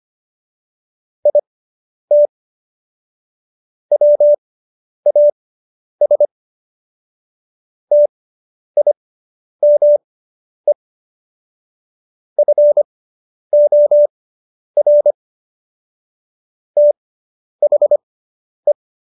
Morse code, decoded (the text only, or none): IT WAS TIME FOR THE